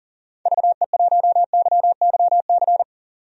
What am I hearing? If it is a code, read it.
Morse code, heard as a keyed tone, about 38 wpm, 700 Hz. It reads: VE1YYC
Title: DXCC-100 Call Signs - 6 Characters - 40wpm